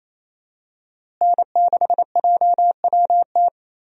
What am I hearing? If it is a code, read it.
Morse code, heard as a keyed tone, about 28 wpm, 700 Hz. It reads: N6JWT
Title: USA Call Signs - 5 Characters - Repeated 2x - 28wpm